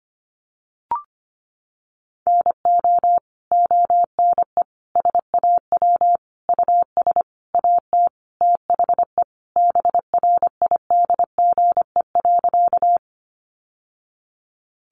Morse code, heard as a keyed tone, about 25 wpm, 700 Hz. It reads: NO ONE SAW US AT THE BRIDGE.
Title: Sentences from Top 1200 Words 25wpm